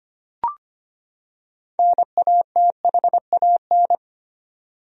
Morse code, heard as a keyed tone, about 25 wpm, 700 Hz. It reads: NATHAN
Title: USA Common Names - 25wpm